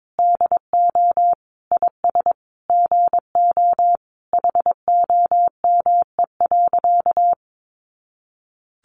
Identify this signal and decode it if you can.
Morse code, heard as a keyed tone, about 22 wpm, 700 Hz. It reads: DO IS GO HOME.
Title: Sentences from Top 200 Words 22wpm